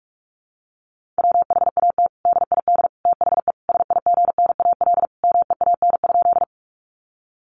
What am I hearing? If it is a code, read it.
Morse code, heard as a keyed tone, about 44 wpm, 700 Hz. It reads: WHAT DID THE SIGNAL MEAN?
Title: Sentences from Top 1400 Words 45wpm